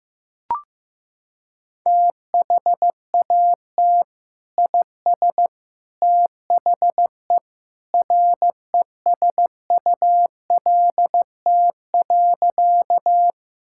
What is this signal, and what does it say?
Morse code, heard as a keyed tone, about 15 wpm, 700 Hz. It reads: THAT IS THE RESULT.
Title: Sentences from Top 1000 Words - Review Part 5 - 15wpm